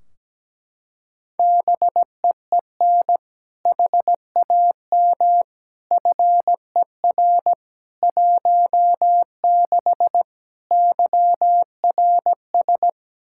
Morse code, heard as a keyed tone, about 17 wpm, 700 Hz. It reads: BEEN HAM FER 16 YRS
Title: Practice QSO - Exchange 3 - More Info - 17wpm